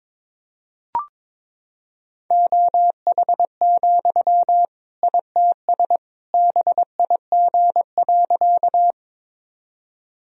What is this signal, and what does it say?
Morse code, heard as a keyed tone, about 21 wpm, 700 Hz. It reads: OH, ITS BIG.
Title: Sentences from Top 1000 Words - Review Part 1 - 22wpm